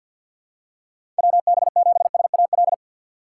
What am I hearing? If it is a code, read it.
Morse code, heard as a keyed tone, about 49 wpm, 700 Hz. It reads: WB7SAL
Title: USA Call Signs - 6x Character Spacing - 50wpm